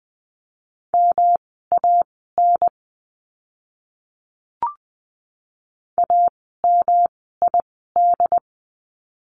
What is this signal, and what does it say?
Morse code, heard as a keyed tone, about 20 wpm, spaced as for 15 wpm, 700 Hz. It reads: MAN AMID
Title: 108 Lesson 18f - Letter m - Words 20wpm